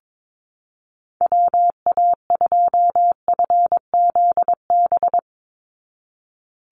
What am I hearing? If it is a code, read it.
Morse code, heard as a keyed tone, about 22 wpm, 700 Hz. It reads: WA2FZB
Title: Call Signs Repeated 2x - 22wpm